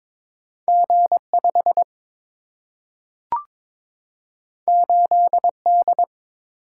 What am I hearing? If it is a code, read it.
Morse code, heard as a keyed tone, about 22 wpm, 700 Hz. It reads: G5 8D
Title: Single-Letter Single-Number 22wpm